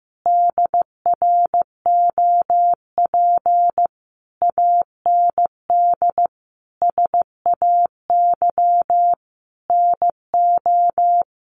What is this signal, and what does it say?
Morse code, heard as a keyed tone, about 15 wpm, 700 Hz. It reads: DROP AND SAY NO
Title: Sentences from Top 1400 Words 15wpm